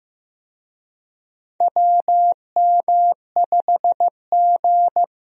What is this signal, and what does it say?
Morse code, heard as a keyed tone, about 15 wpm, 700 Hz. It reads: WM5G
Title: USA Call Signs - 4 Characters - 15wpm